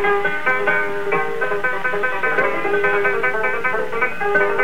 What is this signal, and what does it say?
Emergency Alert System (SAME) tones, heard as harsh buzzing bursts. It is heard from 0.3 to 3.7 s.